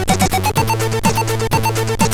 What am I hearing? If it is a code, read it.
There is no code.